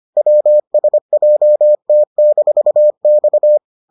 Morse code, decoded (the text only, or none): WSJT-X